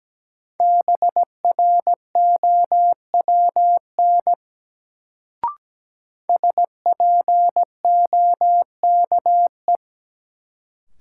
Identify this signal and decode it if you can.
Morse code, heard as a keyed tone, about 17 wpm, 700 Hz. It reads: BROWN SPOKE
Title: Sentences from Top 1200 Words - New Word Drill - 17wpm